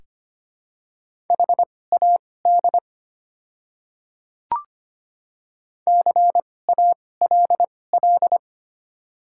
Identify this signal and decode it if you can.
Morse code, heard as a keyed tone, about 25 wpm, spaced as for 18 wpm, 700 Hz. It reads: HAD CALL
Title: 2x Character Spacing - Top 100 Words - 25wpm